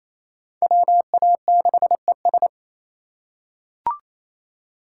Morse code, heard as a keyed tone, about 28 wpm, 700 Hz. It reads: WA6ES
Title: USA Call Signs - 5 Characters - 4x Character Spacing - 28wpm